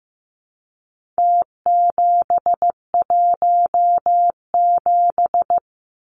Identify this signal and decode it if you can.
Morse code, heard as a keyed tone, about 15 wpm, 700 Hz. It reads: T717
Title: Numbers - Contest Style - 4-Digit 15wpm